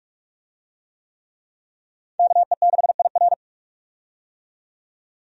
Morse code, heard as a keyed tone, about 46 wpm, 700 Hz. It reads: KE6IR
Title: USA Call Signs - 5 Characters - 45wpm